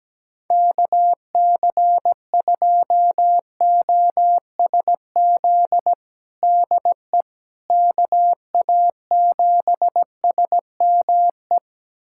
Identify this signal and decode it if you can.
Morse code, heard as a keyed tone, about 17 wpm, 700 Hz. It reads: KC2OSZ DE KA7SME
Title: USA Callsign DE Callsign - 6 Characters - 17wpm